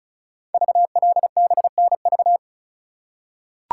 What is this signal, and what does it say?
Morse code, heard as a keyed tone, about 36 wpm, 690 Hz. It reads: VLBNV
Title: Code Groups - 5 Letter:Numbers - 35wpm